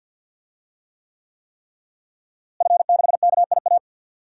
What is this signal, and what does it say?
Morse code, heard as a keyed tone, about 52 wpm, 700 Hz. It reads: F6KIA
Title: DXCC-100 Call Signs - 5 Characters - Repeated 3x - 50wpm